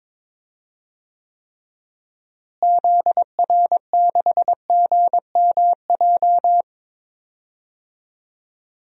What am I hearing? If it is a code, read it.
Morse code, heard as a keyed tone, about 22 wpm, 700 Hz. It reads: ZR6GMJ
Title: DXCC-100 Call Signs - 6 Characters - Repeated 3x - 22wpm